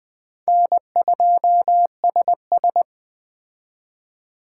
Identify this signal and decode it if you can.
Morse code, heard as a keyed tone, about 20 wpm, 700 Hz. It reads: N2SS